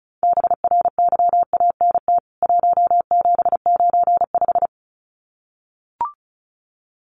Morse code, heard as a keyed tone, about 34 wpm, 700 Hz. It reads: BRYANT 1795
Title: CWT Exchange 35wpm